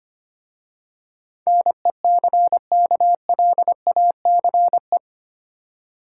Morse code, heard as a keyed tone, about 25 wpm, 700 Hz. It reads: NECKLACE
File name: practice / 25wpm / Easy Words with Hints - 25wpm